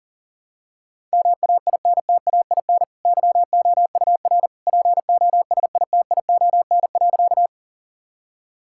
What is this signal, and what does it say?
Morse code, heard as a keyed tone, about 37 wpm, 700 Hz. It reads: MAINTAIN YOUR POSITION.
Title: Sentences from Top 1000 Words - Review Part 5 - 40wpm